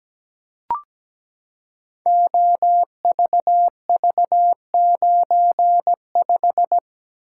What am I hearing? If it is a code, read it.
Morse code, heard as a keyed tone, about 17 wpm, 700 Hz. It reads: OVV95
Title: Code Groups - 5 Letter:Numbers - 17wpm